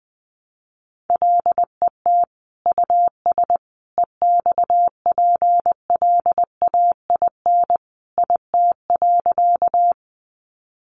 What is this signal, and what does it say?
Morse code, heard as a keyed tone, about 20 wpm, 700 Hz. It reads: LET US EXPLAIN IT.
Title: Sentences from Top 1100 Words 20wpm